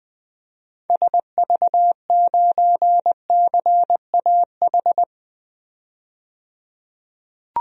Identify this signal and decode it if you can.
Morse code, heard as a keyed tone, about 20 wpm, 700 Hz. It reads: SV9CAH
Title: DXCC-100 Call Signs - 6 Characters - 20wpm